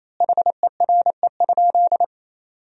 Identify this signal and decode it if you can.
Morse code, heard as a keyed tone, about 28 wpm, 700 Hz. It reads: HERE?
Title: 3x Word-Spacing - Sentences from Top 300 Words - 28wpm